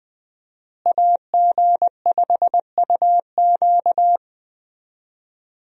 Morse code, heard as a keyed tone, about 21 wpm, 700 Hz. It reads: AG5UQ
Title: USA Call Signs - 5 Characters - Repeated 3x - 20wpm